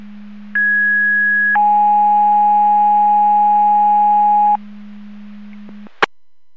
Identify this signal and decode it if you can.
Two-tone page, heard as a one-second tone, then a three-second tone, about 1650 then 830 Hz. It is heard from 0.5 to 4.6 s.
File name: Portage / 00059 11-02 Portage 154.385 Paging